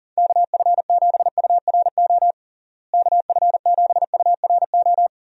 Morse code, heard as a keyed tone, about 40 wpm, 700 Hz. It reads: KF7URO KF7URO
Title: USA Call Signs - 6 Characters - Repeated 2x - 40wpm